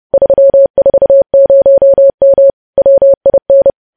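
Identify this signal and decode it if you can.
Morse code, heard as a keyed tone, about 30 wpm, 550 Hz. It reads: 340M WIN
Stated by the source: Morse news headlines sent at 30.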